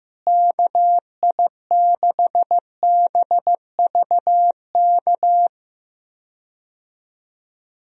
Morse code, heard as a keyed tone, about 16 wpm, 700 Hz. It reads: KI6BVK
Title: USA Call Signs - 6 Characters - Repeated 3x - 15wpm